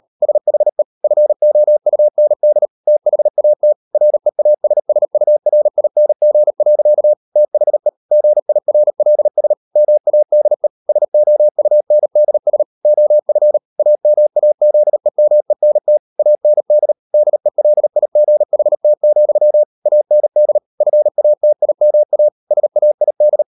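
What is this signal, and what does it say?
Morse code, heard as a keyed tone, about 38 wpm, 600 Hz. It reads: SHE FOUND THAT REASSURING. THE GIRLS MADE SOUNDS OF AMAZEMENT AND DELIGHT, AND FATIMA SAID
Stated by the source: Morse code practice at 38 WPM, sending Ken Follett's novel Never in English, from CW med Gnister.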